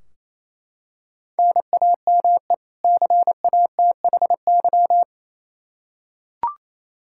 Morse code, heard as a keyed tone, about 28 wpm, 700 Hz. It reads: NAME CATHY
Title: Practice QSO - Exchange 1 - Beginning - 28wpm